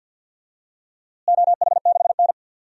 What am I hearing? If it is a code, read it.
Morse code, heard as a keyed tone, about 53 wpm, 700 Hz. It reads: OH6N
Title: DXCC-100 Call Signs - 4 Characters - Repeated 3x - 50wpm